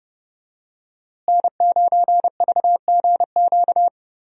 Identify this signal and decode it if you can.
Morse code, heard as a keyed tone, about 32 wpm, 700 Hz. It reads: N9VGQ
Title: USA Call Signs - 5 Characters - Repeated 2x - 30wpm